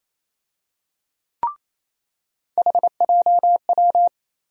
Morse code, heard as a keyed tone, about 27 wpm, 700 Hz. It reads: HJW
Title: USA Call Sign Postfixes - 3 Letters 28wpm